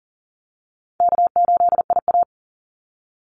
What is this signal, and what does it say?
Morse code, heard as a keyed tone, about 40 wpm, 700 Hz. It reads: K8IA
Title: Easy Call Signs Repeated 3x - 40wpm